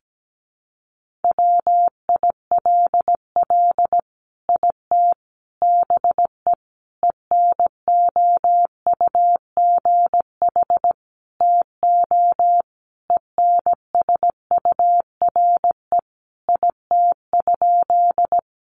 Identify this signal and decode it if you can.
Morse code, heard as a keyed tone, about 17 wpm, 700 Hz. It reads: WILL IT BE ENOUGH TO ENSURE IT?
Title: Sentences from Top 1100 Words 17wpm